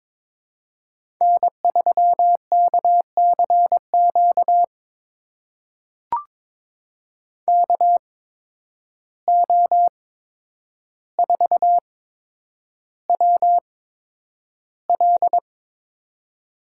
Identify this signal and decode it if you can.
Morse code, heard as a keyed tone, about 22 wpm, spaced as for 9 wpm, 700 Hz. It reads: N3KCQ KO4WL
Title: USA Call Signs - 5 Characters - 8x Character Spacing - 22wpm